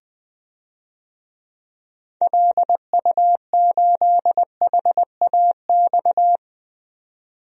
Morse code, heard as a keyed tone, about 20 wpm, 700 Hz. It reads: LU8HAX
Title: DXCC-100 Call Signs - 6 Characters - 20wpm